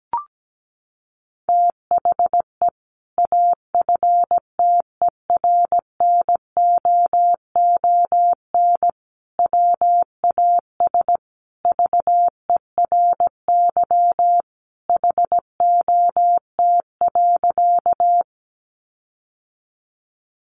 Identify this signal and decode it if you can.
Morse code, heard as a keyed tone, about 17 wpm, 700 Hz. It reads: THE AFTERNOON WAS VERY HOT.
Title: Sentences from Top 1500 Words 17wpm